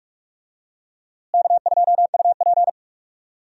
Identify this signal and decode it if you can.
Morse code, heard as a keyed tone, about 46 wpm, 700 Hz. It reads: K2UP
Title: USA Call Signs - 4 Characters - Repeated 3x - 45wpm